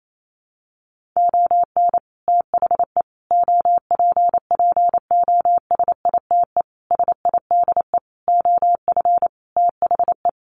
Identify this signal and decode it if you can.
Morse code, heard as a keyed tone, about 28 wpm, 700 Hz. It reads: ON THE OPPOSITE SIDE OF THE
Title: Sentences from Top 1400 Words 28wpm